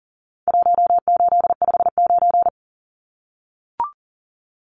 Morse code, heard as a keyed tone, about 40 wpm, 700 Hz. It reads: 1859
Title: Numbers - 4-Digit 40wpm